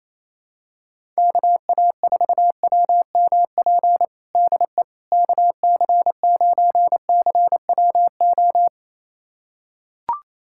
Morse code, heard as a keyed tone, about 28 wpm, 700 Hz. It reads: KA4WMP DE KC9CWO
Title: USA Callsign DE Callsign - 6 Characters - 28wpm